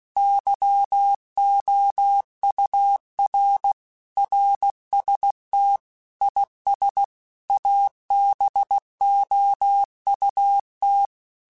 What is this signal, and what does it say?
Morse code, heard as a keyed tone, about 16 wpm, 800 Hz. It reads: YOUR RST IS ABOUT
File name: QSOs / QSOs-mp3s / WD0FBT 1@15